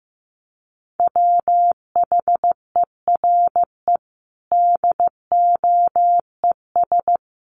Morse code, heard as a keyed tone, about 14 wpm, 700 Hz. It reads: WHERE DOES
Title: Sentences from Top 1400 Words 15wpm